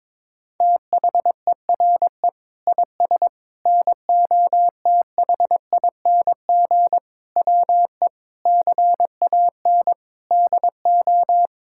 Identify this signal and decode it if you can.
Morse code, heard as a keyed tone, about 22 wpm, 700 Hz. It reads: THERE IS NOTHING WE CAN DO
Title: Sentences from Top 1000 Words - Review Part 6 - 22wpm